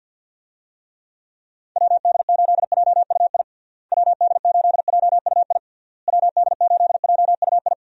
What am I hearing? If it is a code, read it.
Morse code, heard as a keyed tone, about 48 wpm, 700 Hz. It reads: WD8JUI WD8JUI WD8JUI
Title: USA Call Signs - 6 Characters - Repeated 3x - 50wpm